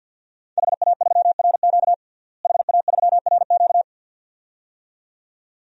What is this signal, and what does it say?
Morse code, heard as a keyed tone, about 53 wpm, 700 Hz. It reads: HA3RQ HA3RQ
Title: DXCC-100 Call Signs - 5 Characters - Repeated 2x - 50wpm